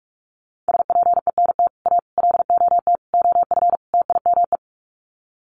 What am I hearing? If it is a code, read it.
Morse code, heard as a keyed tone, about 44 wpm, 700 Hz. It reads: SPENT A LOT OF TIME
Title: Sets of 5 Words 45wpm